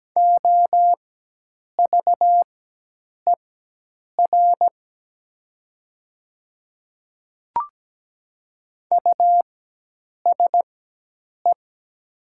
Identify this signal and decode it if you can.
Morse code, heard as a keyed tone, about 17 wpm, spaced as for 8 wpm, 700 Hz. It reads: OVER USE
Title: 4x Character Spacing - Top 100 Words - 17wpm